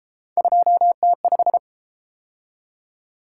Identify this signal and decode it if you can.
Morse code, heard as a keyed tone, about 32 wpm, 700 Hz. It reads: 2T5